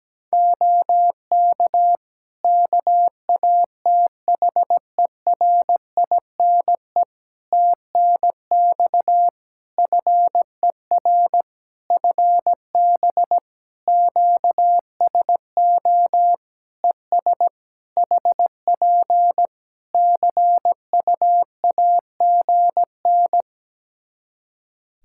Morse code, heard as a keyed tone, about 17 wpm, 700 Hz. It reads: OK KATHERINE TNX FER FB QSO ES HP CUAGN
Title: K1USN SST QSO Practice 17wpm